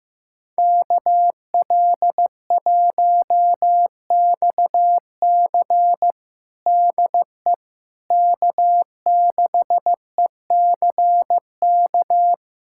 Morse code, heard as a keyed tone, about 15 wpm, 700 Hz. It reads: KL1XC DE K6ECK
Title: USA Callsign DE Callsign - 5 Characters - 15wpm